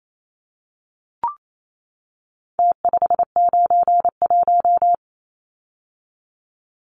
Morse code, heard as a keyed tone, about 28 wpm, 700 Hz. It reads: T591
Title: Numbers - Contest Style - 4-Digit 28wpm